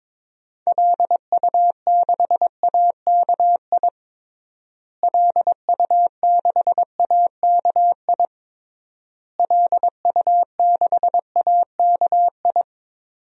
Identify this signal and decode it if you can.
Morse code, heard as a keyed tone, about 22 wpm, 700 Hz. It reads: LU6AKI LU6AKI LU6AKI
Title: DXCC-100 Call Signs - 6 Characters - Repeated 3x - 22wpm